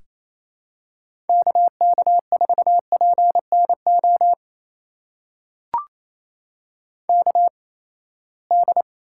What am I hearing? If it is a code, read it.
Morse code, heard as a keyed tone, about 28 wpm, spaced as for 19 wpm, 700 Hz. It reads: KK4PNO KD
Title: USA Call Signs - 6 Characters - 8x Character Spacing - 28wpm